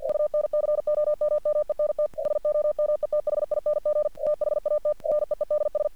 Morse code, heard as a keyed tone, about 47 wpm, 600 Hz. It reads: UNCOMMENT SOMETHING THAT NEEDS
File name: archived / CRSnet